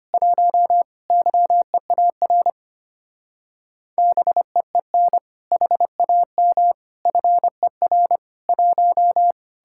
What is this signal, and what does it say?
Morse code, heard as a keyed tone, about 30 wpm, 700 Hz. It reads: 1 YEAR BEEN HAM FER 1
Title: Speed-Racing Common QSO Phrases - 25wpm